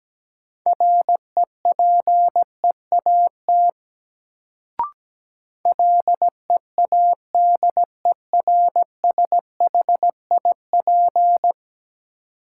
Morse code, heard as a keyed tone, about 17 wpm, 700 Hz. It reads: REPEAT LEADERSHIP
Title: Sentences from Top 1600 Words - New Word Drill - 17wpm